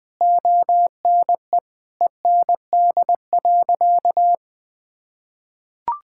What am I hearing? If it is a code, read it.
Morse code, heard as a keyed tone, about 20 wpm, 700 Hz. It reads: ONE END.
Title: Sentences from Top 1600 Words 20wpm